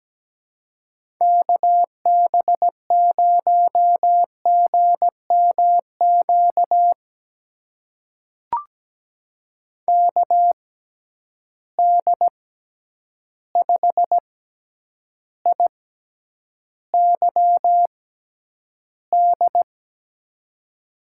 Morse code, heard as a keyed tone, about 17 wpm, spaced as for 9 wpm, 700 Hz. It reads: KB0GMQ KD5IYD
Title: USA Call Signs - 6 Characters - 6x Character Spacing - 17wpm